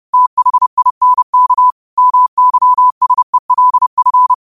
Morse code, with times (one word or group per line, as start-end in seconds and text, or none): 0.1-1.7 THINK
2.0-4.3 MYSELF